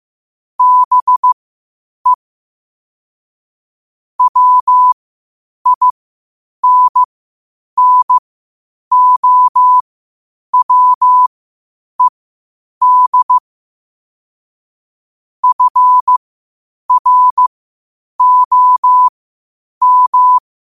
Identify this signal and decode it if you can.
Morse code, heard as a keyed tone, about 15 wpm, spaced as for 8 wpm, 1000 Hz. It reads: BE WINNOWED FROM
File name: qotdfiles / 8wpm